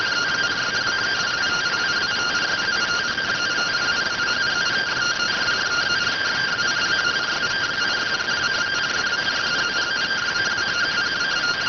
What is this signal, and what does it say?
Two-tone teleprinter signal, about 170 Hz shift.